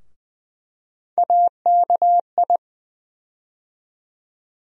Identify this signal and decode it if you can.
Morse code, heard as a keyed tone, about 21 wpm, 700 Hz. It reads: AKI
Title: USA Call Sign Postfixes - 3 Letters 20wpm